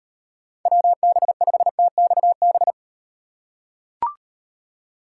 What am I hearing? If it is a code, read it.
Morse code, heard as a keyed tone, about 38 wpm, 700 Hz. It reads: WB5TXB